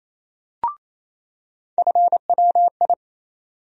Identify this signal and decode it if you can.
Morse code, heard as a keyed tone, about 28 wpm, 700 Hz. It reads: FWI
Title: USA Call Sign Postfixes - 3 Letters 28wpm